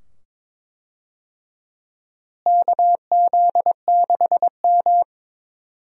Morse code, heard as a keyed tone, about 21 wpm, 700 Hz. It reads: KZ6M
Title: USA Call Signs - 4 Characters - Repeated 2x - 22wpm